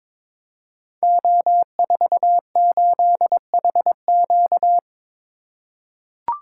Morse code, heard as a keyed tone, about 22 wpm, 700 Hz. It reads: O48HQ
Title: Code Groups - 5 Letter:Numbers - 22wpm